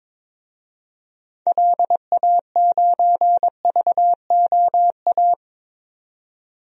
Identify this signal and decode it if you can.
Morse code, heard as a keyed tone, about 22 wpm, 700 Hz. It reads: LA9VOA
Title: DXCC-100 Call Signs - 6 Characters - 22wpm